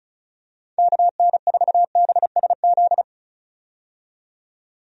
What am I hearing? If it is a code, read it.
Morse code, heard as a keyed tone, about 36 wpm, 700 Hz. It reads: KN4BSZ
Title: USA Call Signs - 6 Characters - 35wpm